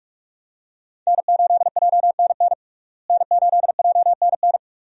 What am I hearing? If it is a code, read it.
Morse code, heard as a keyed tone, about 43 wpm, 700 Hz. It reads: N8JNN N8JNN
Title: USA Call Signs - 5 Characters - Repeated 2x - 45wpm